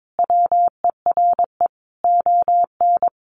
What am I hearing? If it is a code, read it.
Morse code, heard as a keyed tone, about 22 wpm, 700 Hz. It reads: WERE ON
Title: Sentences from Top 1400 Words 22wpm